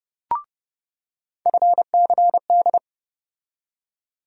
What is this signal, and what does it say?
Morse code, heard as a keyed tone, about 30 wpm, 700 Hz. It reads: FCD
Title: USA Call Sign Postfixes - 3 Letters 30wpm